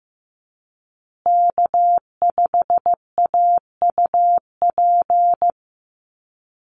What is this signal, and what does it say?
Morse code, heard as a keyed tone, about 15 wpm, 700 Hz. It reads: K5AUP
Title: Call Signs Repeated 2x - 15wpm